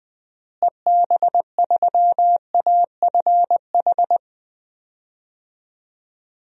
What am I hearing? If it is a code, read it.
Morse code, heard as a keyed tone, about 20 wpm, 700 Hz. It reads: EB3AFH
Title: DXCC-100 Call Signs - 6 Characters - Repeated 2x - 20wpm